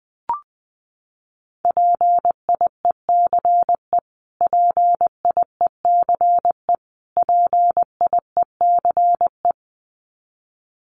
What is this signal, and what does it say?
Morse code, heard as a keyed tone, about 20 wpm, 700 Hz. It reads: PIECE PIECE PIECE
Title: Top 500 Words Repeated 20wpm